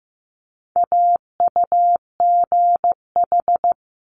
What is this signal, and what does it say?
Morse code, heard as a keyed tone, about 15 wpm, 700 Hz. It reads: AUGH
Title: Multi-letter Phonemes Quick Fire 15wpm